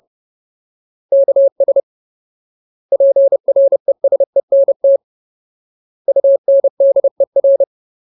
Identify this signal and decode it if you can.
Morse code, heard as a keyed tone, about 30 wpm, 550 Hz. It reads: KS PRESENT UNDER
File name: Ultimate Word List / 30WPM Ultimate